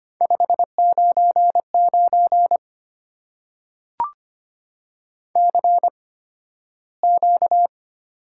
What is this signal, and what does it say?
Morse code, heard as a keyed tone, about 25 wpm, spaced as for 13 wpm, 700 Hz. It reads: 599 CQ
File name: Top 90 QSO Elements - 8x Character Spacing - 25wpm